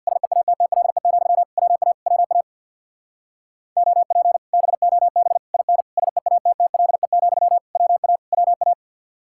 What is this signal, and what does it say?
Morse code, heard as a keyed tone, about 61 wpm, 700 Hz. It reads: SEATTLE, WA WA OP BOB IN SEATTLE, WA WA
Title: Speed-Racing Common QSO Phrases - 50wpm